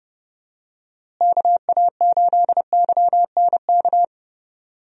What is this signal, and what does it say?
Morse code, heard as a keyed tone, about 32 wpm, 700 Hz. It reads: KA8YNK
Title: USA Call Signs - 6 Characters - Repeated 3x - 30wpm